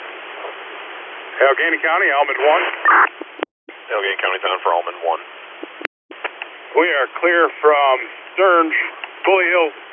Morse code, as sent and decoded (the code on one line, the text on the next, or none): none